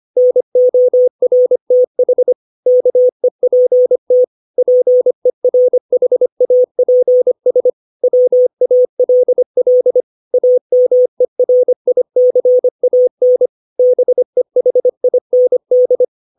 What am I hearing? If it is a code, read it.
Morse code, heard as a keyed tone, about 25 wpm, 500 Hz. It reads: NORTH KEPT PERHAPS WALL AMERICAN BEHIND
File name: 632 words / 632 Words 25WPM